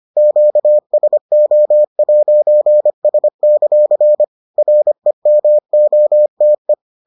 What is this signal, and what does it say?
Morse code, heard as a keyed tone, about 25 wpm, 600 Hz. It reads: QSO'S; REMOTE